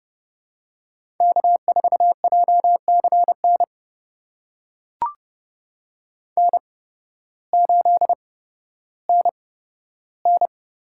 Morse code, heard as a keyed tone, about 30 wpm, spaced as for 14 wpm, 700 Hz. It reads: K4JCN N8NN
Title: USA Call Signs - 8x Character Spacing - 30wpm